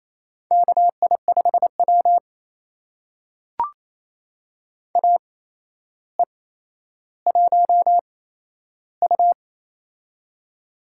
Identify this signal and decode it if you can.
Morse code, heard as a keyed tone, about 28 wpm, spaced as for 12 wpm, 700 Hz. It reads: KI5W AE1U
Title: USA Call Signs - 4 Characters - 8x Character Spacing - 28wpm